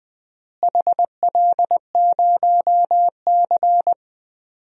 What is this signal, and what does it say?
Morse code, heard as a keyed tone, about 21 wpm, 700 Hz. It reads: HL0C
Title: DXCC-100 Call Signs - 4 Characters - Repeated 2x - 20wpm